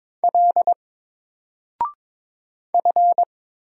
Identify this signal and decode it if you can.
Morse code, heard as a keyed tone, about 22 wpm, 700 Hz. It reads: L F
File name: Single-Letter-Number-22wpm